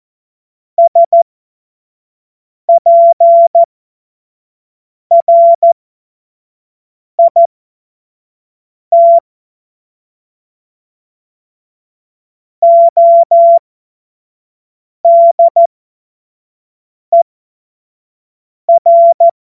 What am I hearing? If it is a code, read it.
Morse code, heard as a keyed tone, about 14 wpm, spaced as for 5 wpm, 680 Hz. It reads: SPRIT ODER
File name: Morsekurs / Witze / Witz 30. Mai 2023 25 F70